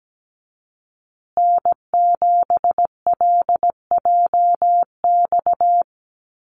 Morse code, heard as a keyed tone, about 17 wpm, 700 Hz. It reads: N7LJX
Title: USA Call Signs Repeated 2x 17wpm